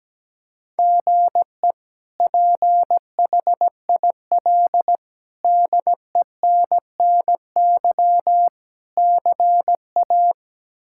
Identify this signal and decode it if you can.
Morse code, heard as a keyed tone, about 17 wpm, 700 Hz. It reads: GE PHIL DENNY CA
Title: K1USN SST QSO Practice - 17wpm